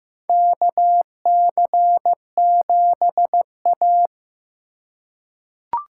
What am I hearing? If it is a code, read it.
Morse code, heard as a keyed tone, about 15 wpm, 700 Hz. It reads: KC7A
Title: USA Call Signs - 4 Characters - 4x Character Spacing - 15wpm